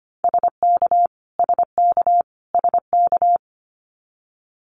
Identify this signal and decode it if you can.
Morse code, heard as a keyed tone, about 25 wpm, 700 Hz. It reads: SK SK SK
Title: Canadian Provinces and Territories - Repeated 3x - 25wpm